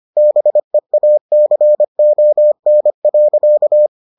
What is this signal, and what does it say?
Morse code, heard as a keyed tone, about 25 wpm, 600 Hz. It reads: BEACON.